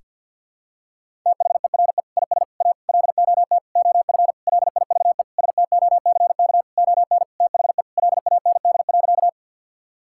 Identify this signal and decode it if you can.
Morse code, heard as a keyed tone, about 50 wpm, 700 Hz. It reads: THERE IS A LOT OF LIVE STOCK ON THE LAND.
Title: Sentences from Top 1000 Words - Review Part 5 - 50wpm